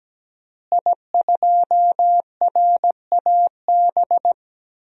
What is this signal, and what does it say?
Morse code, heard as a keyed tone, about 17 wpm, 700 Hz. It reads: I2RAB
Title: DXCC-100 Call Signs - 5 Characters - Repeated 3x - 17wpm